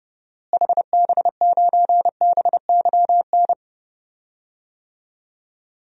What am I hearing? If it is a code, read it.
Morse code, heard as a keyed tone, about 30 wpm, 700 Hz. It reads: HB9BYN
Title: DXCC-100 Call Signs - 6 Characters - Repeated 2x - 30wpm